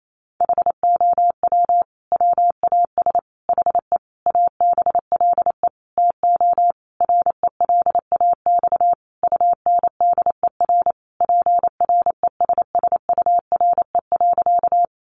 Morse code, heard as a keyed tone, about 28 wpm, 700 Hz. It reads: HOW WAS HE ABLE TO RELAX UNDER PRESSURE.
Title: Sentences from Top 1400 Words 28wpm